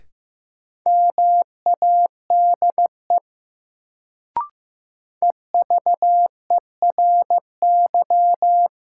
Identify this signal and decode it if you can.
Morse code, heard as a keyed tone, about 15 wpm, 700 Hz. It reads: MADE EVERY
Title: Sentences from Top 200 Words - New Word Drill - 15wpm